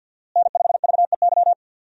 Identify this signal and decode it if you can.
Morse code, heard as a keyed tone, about 52 wpm, 700 Hz. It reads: N5VEY
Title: USA Call Signs - 5 Characters - Repeated 3x - 50wpm